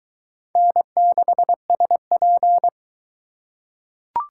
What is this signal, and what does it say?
Morse code, heard as a keyed tone, about 20 wpm, 700 Hz. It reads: N6SP